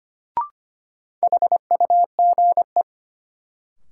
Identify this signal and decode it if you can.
Morse code, heard as a keyed tone, about 25 wpm, 700 Hz. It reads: HUGE
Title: Sentences from Top 1100 Words - New Word Drill Alt - 25wpm